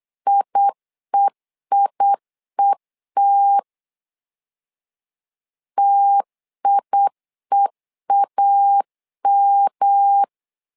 Morse code, heard as a keyed tone, about 8 wpm, 800 Hz. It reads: IEIET TIEAM
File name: LEZIONI / 3 MWRO / 3 M 8wpm 2300